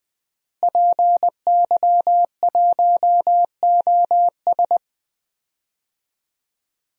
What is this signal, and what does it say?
Morse code, heard as a keyed tone, about 21 wpm, 700 Hz. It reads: PY1OS
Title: DXCC-100 Call Signs - 5 Characters - Repeated 3x - 20wpm